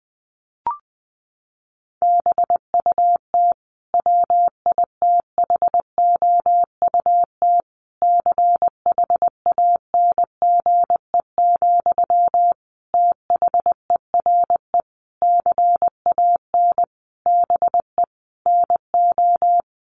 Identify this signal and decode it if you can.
Morse code, heard as a keyed tone, about 20 wpm, 700 Hz. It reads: BUT WITHOUT CHANGE, THERE CAN BE NO
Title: Sentences from Top 1400 Words 20wpm